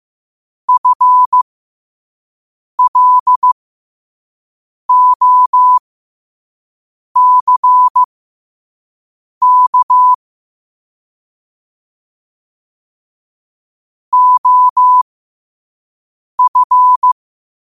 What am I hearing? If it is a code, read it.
Morse code, heard as a keyed tone, about 15 wpm, spaced as for 5 wpm, 1000 Hz. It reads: FLOCK OF
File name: qotdfiles / 5wpm